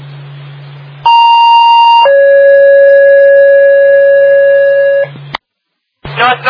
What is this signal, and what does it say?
Two-tone page, heard as a one-second tone, then a three-second tone, about 930 then 560 Hz, with speech. It is heard from 1.1 to 5.0 s.